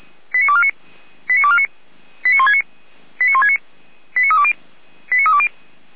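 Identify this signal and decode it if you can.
Five-tone selective call (ZVEI), heard as quick runs of five tones, tones of about 70 ms. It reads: 89249 89249 89169 89169 89320 89320
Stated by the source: SDR-received ZVEI five-tone selective call.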